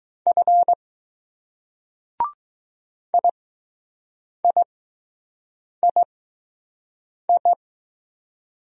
Morse code, heard as a keyed tone, about 20 wpm, spaced as for 6 wpm, 700 Hz. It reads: F IIII